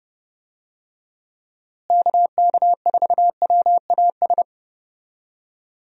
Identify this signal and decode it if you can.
Morse code, heard as a keyed tone, about 32 wpm, 700 Hz. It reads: KK4WAS
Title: USA Call Signs - 6 Characters - Repeated 3x - 30wpm